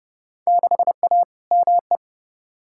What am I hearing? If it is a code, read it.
Morse code, heard as a keyed tone, about 30 wpm, 700 Hz. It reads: 6A ME
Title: Field Day Exchanges - 30wpm